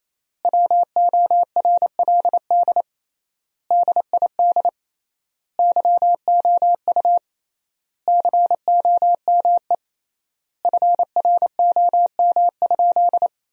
Morse code, heard as a keyed tone, about 27 wpm, 700 Hz. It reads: WORLD DID YOU COME FROM?
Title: 3x Word-Spacing - Sentences from Top 200 Words - 28wpm